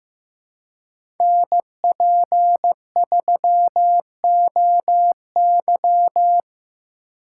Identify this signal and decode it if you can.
Morse code, heard as a keyed tone, about 15 wpm, 700 Hz. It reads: NP3OY
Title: USA Call Signs - 5 Characters - 8x Character Spacing - 15wpm